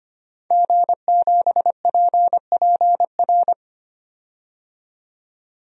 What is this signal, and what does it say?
Morse code, heard as a keyed tone, about 25 wpm, 700 Hz. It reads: G7PPR
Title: DXCC-100 Call Signs - 5 Characters - 25wpm